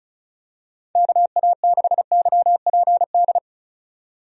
Morse code, heard as a keyed tone, about 36 wpm, 700 Hz. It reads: KA6YPD
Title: USA Call Signs - 6 Characters - 35wpm